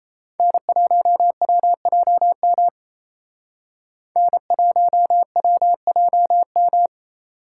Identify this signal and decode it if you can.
Morse code, heard as a keyed tone, about 35 wpm, 700 Hz. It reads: N1WJM N1WJM
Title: Speed Racing - USA Call Signs 28wpm